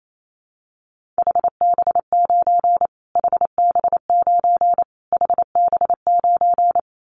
Morse code, heard as a keyed tone, about 28 wpm, 700 Hz. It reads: HB9 HB9 HB9
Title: Callsign Prefixes for Top 25 DXCC - Repeated 3x - 28wpm